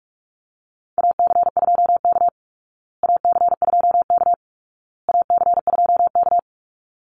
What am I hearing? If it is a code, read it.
Morse code, heard as a keyed tone, about 44 wpm, 700 Hz. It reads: AC2K AC2K AC2K
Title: Call Signs Repeated 2x - 45wpm